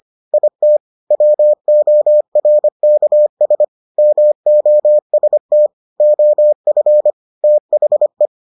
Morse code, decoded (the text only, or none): IT WORKS MOST OF THE